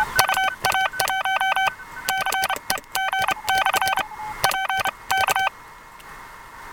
Morse code, heard as a keyed tone, about 29 wpm, spaced as for 21 wpm, 770 Hz, with noise. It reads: UA1CEG/P K